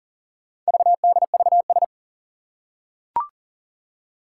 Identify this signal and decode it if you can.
Morse code, heard as a keyed tone, about 40 wpm, 700 Hz. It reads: VDVS
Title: Four Challenging Letter-Number Combinations 40wpm